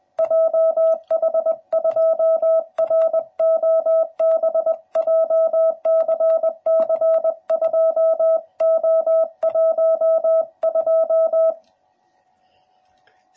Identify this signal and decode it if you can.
Morse code, heard as a keyed tone, about 20 wpm, 650 Hz. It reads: JH2ROB JCC2O12
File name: jh8cbh / 2021cw / cw12b